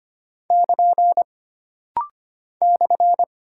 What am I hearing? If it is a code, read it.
Morse code, heard as a keyed tone, about 25 wpm, 700 Hz. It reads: <KN> /
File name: Single Punctuation-Prosign - Rapid-Fire - 25wpm